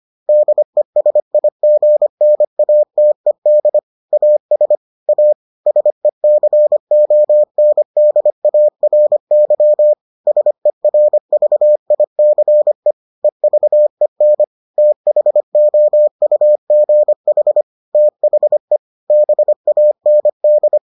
Morse code, heard as a keyed tone, about 25 wpm, 600 Hz. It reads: DESIGNATED AS A SECONDARY SERVICE EVEN THOUGH THE BAND